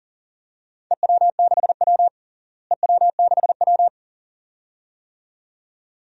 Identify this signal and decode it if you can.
Morse code, heard as a keyed tone, about 40 wpm, 700 Hz. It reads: EW6W EW6W
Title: DXCC-100 Call Signs - 4 Characters - Repeated 2x - 40wpm